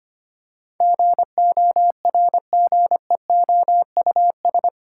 Morse code, heard as a keyed tone, about 25 wpm, 700 Hz. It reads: GORGEOUS
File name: 8 Letter Words - First 3 Letters Given - 25wpm